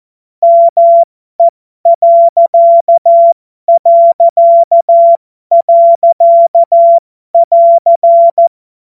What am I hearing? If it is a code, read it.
Morse code, heard as a keyed tone, about 14 wpm, spaced as for 12 wpm, 680 Hz. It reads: ME...<AR>
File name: Morsekurs / Witze